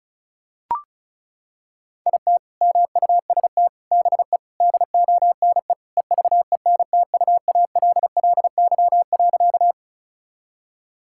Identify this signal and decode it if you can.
Morse code, heard as a keyed tone, about 36 wpm, 700 Hz. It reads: IT MUST BE DONE EVENTUALLY.
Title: Sentences from Top 1000 Words - Review Part 6 - 35wpm